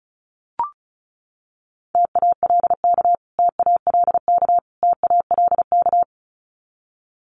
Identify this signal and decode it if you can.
Morse code, heard as a keyed tone, about 34 wpm, 700 Hz. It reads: TALK TALK TALK
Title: Top 500 Words Repeated 35wpm